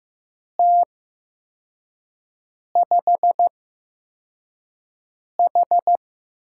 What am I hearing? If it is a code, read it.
Morse code, heard as a keyed tone, about 15 wpm, spaced as for 4 wpm, 700 Hz. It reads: T5H
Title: USA Call Signs - 8x Character Spacing - 15wpm